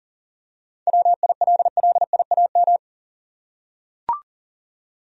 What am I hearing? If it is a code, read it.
Morse code, heard as a keyed tone, about 40 wpm, 700 Hz. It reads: WILLIAM
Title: USA Common Names - 40wpm